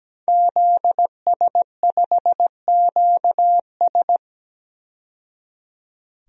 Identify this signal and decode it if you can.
Morse code, heard as a keyed tone, about 17 wpm, 700 Hz. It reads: ZS5QS
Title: DXCC-100 Call Signs - 5 Characters - 17wpm